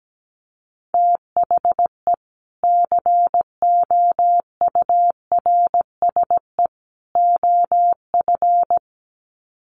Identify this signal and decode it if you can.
Morse code, heard as a keyed tone, about 17 wpm, 700 Hz. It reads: THE COURSE OF
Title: Sets of 3 Words 17wpm